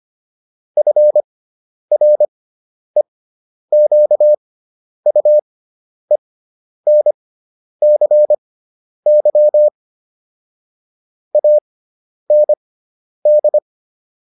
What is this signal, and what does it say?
Morse code, heard as a keyed tone, about 25 wpm, spaced as for 10 wpm, 600 Hz. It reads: FREQUENCY AND